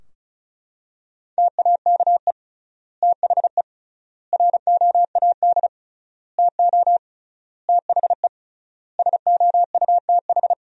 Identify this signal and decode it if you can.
Morse code, heard as a keyed tone, about 34 wpm, 700 Hz. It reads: TAKE THE ROAD TO THE SOUTH
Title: 3x Word-Spacing - Sentences from Top 400 Words - 35wpm